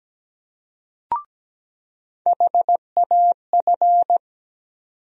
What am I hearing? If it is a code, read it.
Morse code, heard as a keyed tone, about 17 wpm, 700 Hz. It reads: HAF
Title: USA Call Sign Postfixes - 3 Letters 17wpm